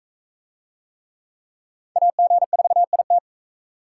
Morse code, heard as a keyed tone, about 42 wpm, 700 Hz. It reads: AG4IT